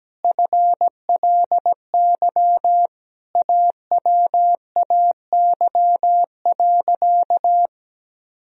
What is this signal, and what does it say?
Morse code, heard as a keyed tone, about 17 wpm, 700 Hz. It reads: FLY AWAY.
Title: Sentences from Top 1000 Words - Review Part 2 - 17wpm